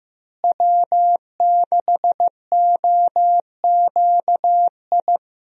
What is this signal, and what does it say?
Morse code, heard as a keyed tone, about 15 wpm, 700 Hz. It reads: W6OQI
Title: USA Call Signs - 5 Characters - Repeated 3x - 15wpm